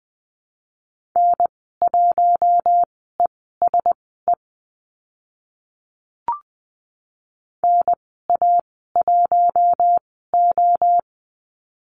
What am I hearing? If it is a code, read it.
Morse code, heard as a keyed tone, about 20 wpm, spaced as for 15 wpm, 700 Hz. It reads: N1ESE NA1O
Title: 076 Lesson 13g - Letter L - Call Signs 20wpm